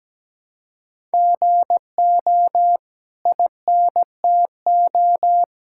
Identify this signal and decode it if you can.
Morse code, heard as a keyed tone, about 17 wpm, 700 Hz. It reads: GO INTO
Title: Sentences from Top 1000 Words - Review Part 6 - 17wpm